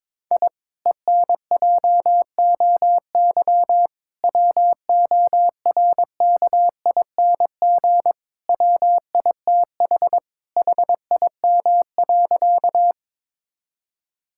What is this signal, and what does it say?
Morse code, heard as a keyed tone, about 22 wpm, 700 Hz. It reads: I ENJOY WORKING WITH HIM.
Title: Sentences from Top 1000 Words - Review Part 5 - 22wpm